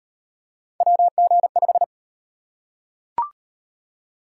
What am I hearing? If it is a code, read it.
Morse code, heard as a keyed tone, about 38 wpm, 700 Hz. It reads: WG5